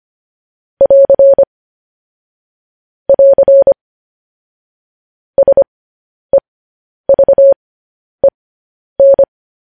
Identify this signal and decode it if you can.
Morse code, heard as a keyed tone, about 25 wpm, spaced as for 10 wpm, 550 Hz. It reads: <AR> <AR> SEVEN
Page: news-headlines-in-morse-code-at-10-wpm